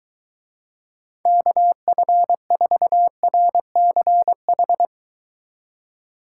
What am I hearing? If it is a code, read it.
Morse code, heard as a keyed tone, about 23 wpm, 700 Hz. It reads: KF4RCH